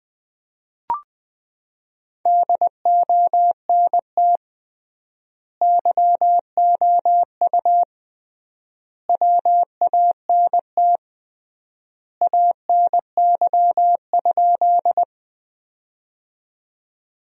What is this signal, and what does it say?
Morse code, heard as a keyed tone, about 20 wpm, 700 Hz. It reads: DONT YOU WANT ANY?
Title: 3x Word-Spacing - Sentences from Top 300 Words - 20wpm